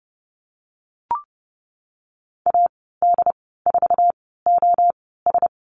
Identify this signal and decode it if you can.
Morse code, heard as a keyed tone, about 30 wpm, spaced as for 17 wpm, 700 Hz. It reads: AD4OS
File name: Lesson 14g - Number 2 - Call Signs 30wpm